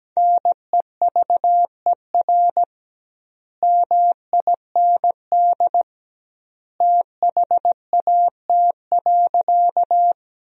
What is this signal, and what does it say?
Morse code, heard as a keyed tone, about 17 wpm, 700 Hz. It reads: NEVER MIND THAT.
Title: 2x Word-Spacing - Sentences from Top 500 Words - 17wpm